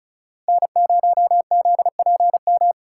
Morse code, heard as a keyed tone, about 46 wpm, 700 Hz. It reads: N0ZPM